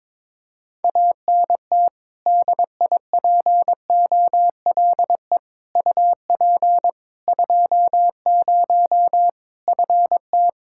Morse code, heard as a keyed tone, about 22 wpm, 700 Hz. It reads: ANT DIPOLE UP 20 FT
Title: Practice QSO - Exchange 2 - Info - 22wpm